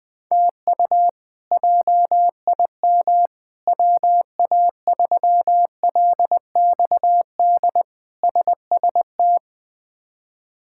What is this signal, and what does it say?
Morse code, heard as a keyed tone, about 20 wpm, 700 Hz. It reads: TU JIM WA3LXD SST
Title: K1USN SST QSO Practice - 20wpm